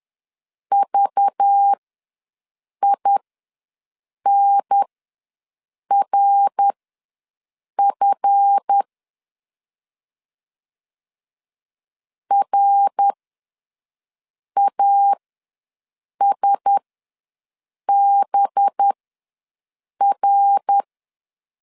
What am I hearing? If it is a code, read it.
Morse code, heard as a keyed tone, about 10 wpm, spaced as for 6 wpm, 800 Hz. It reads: VINRF RASBR
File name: LEZIONI / 7 GF9Z / 7 GF ws 10wpm 3918